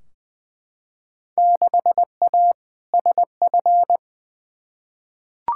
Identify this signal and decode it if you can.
Morse code, heard as a keyed tone, about 20 wpm, 700 Hz. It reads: 6A SF
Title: Field Day Exchanges - 20wpm